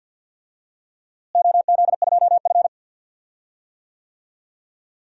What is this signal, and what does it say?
Morse code, heard as a keyed tone, about 53 wpm, 700 Hz. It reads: OZ2F
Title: DXCC-100 Call Signs - 4 Characters - Repeated 3x - 50wpm